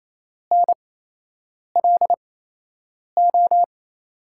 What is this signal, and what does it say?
Morse code, heard as a keyed tone, about 28 wpm, spaced as for 8 wpm, 700 Hz. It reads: NLO
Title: USA Call Signs - 6 Characters - 8x Character Spacing - 28wpm